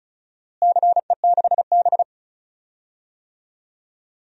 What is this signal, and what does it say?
Morse code, heard as a keyed tone, about 36 wpm, 700 Hz. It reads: CE6B
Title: DXCC-100 Call Signs - 4 Characters - Repeated 2x - 35wpm